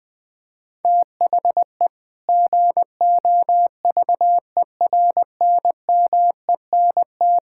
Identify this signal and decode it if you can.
Morse code, heard as a keyed tone, about 20 wpm, 700 Hz. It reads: THE GOVERNMENT
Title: Sentences from Top 1000 Words - Review Part 5 - 20wpm